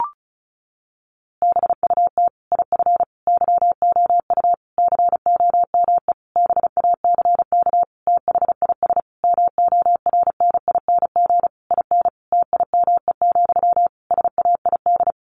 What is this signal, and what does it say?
Morse code, heard as a keyed tone, about 34 wpm, 700 Hz. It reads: BUT IF YOU COME BACK THIS MORNING IN TIME, SAID
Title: A Sound Of Thunder 35wpm